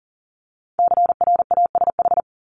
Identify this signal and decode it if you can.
Morse code, heard as a keyed tone, about 42 wpm, 700 Hz. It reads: CRASH
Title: Multi-letter Phonemes 40wpm